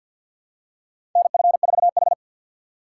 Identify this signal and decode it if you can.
Morse code, heard as a keyed tone, about 53 wpm, 700 Hz. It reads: NF4H
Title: USA Call Signs - 4 Characters - 3x Character Spacing - 50wpm